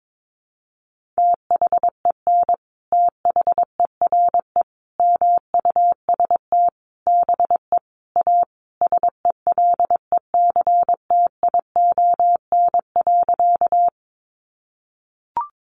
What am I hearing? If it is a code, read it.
Morse code, heard as a keyed tone, about 22 wpm, 700 Hz. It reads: THEN THERE MUST BE A SELECTION.
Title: Sentences from Top 1400 Words 22wpm